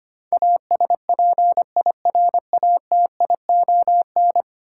Morse code, heard as a keyed tone, about 25 wpm, 700 Hz. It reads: ASPIRATION
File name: 10 Letter Words - First 3 Letters Given - 25wpm